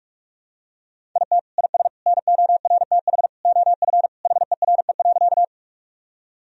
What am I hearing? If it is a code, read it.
Morse code, heard as a keyed tone, about 46 wpm, 700 Hz. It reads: IT IS NORTH OF HERE.
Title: Sentences from Top 1000 Words - Review Part 1 - 45wpm